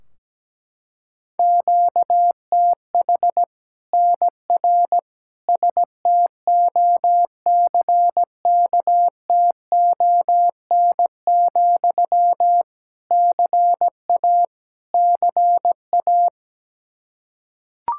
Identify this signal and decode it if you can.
Morse code, heard as a keyed tone, about 17 wpm, 700 Hz. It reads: QTH NR STOCKTON, CA CA
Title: Practice QSO - Exchange 1 - Beginning - 17wpm